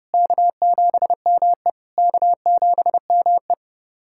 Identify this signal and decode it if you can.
Morse code, heard as a keyed tone, about 30 wpm, 690 Hz. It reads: K7ME K7ME
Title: SOTA QSO Practice 30wpm